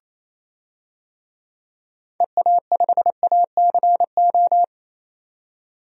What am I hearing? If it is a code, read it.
Morse code, heard as a keyed tone, about 28 wpm, 700 Hz. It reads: EA5ACO
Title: DXCC-100 Call Signs - 6 Characters - 28wpm